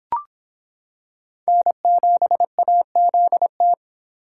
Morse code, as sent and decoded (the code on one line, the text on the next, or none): -. --... .- --.. -
N7AZT